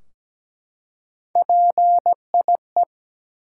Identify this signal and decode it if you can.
Morse code, heard as a keyed tone, about 17 wpm, 700 Hz. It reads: PIE
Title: Three Letter Words - 17wpm